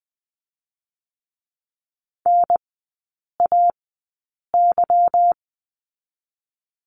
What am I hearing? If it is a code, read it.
Morse code, heard as a keyed tone, about 20 wpm, 700 Hz. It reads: N A Y
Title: 198 Lesson 32e - Letter k - Three Characters 20wpm